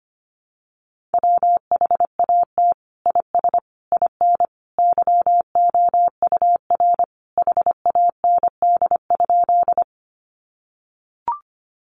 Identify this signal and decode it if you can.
Morse code, heard as a keyed tone, about 25 wpm, 700 Hz. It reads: WHAT IS IN YOUR HAND?
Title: Sentences from Top 200 Words 25wpm